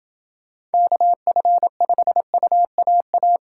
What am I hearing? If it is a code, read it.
Morse code, heard as a keyed tone, about 30 wpm, 700 Hz. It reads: KF5UAA